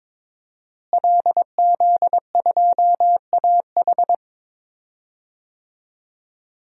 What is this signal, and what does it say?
Morse code, heard as a keyed tone, about 22 wpm, 700 Hz. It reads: LZ2AH
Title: DXCC-100 Call Signs - 5 Characters - Repeated 2x - 22wpm